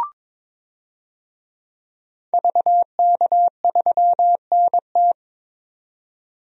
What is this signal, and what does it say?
Morse code, heard as a keyed tone, about 22 wpm, 700 Hz. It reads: VK3NT
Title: DXCC-100 Call Signs - 5 Characters - 22wpm